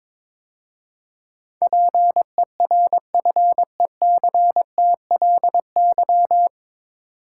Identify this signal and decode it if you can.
Morse code, heard as a keyed tone, about 22 wpm, 700 Hz. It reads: PERFECTLY
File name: 9 Letter Words - First 3 Letters Given - 22wpm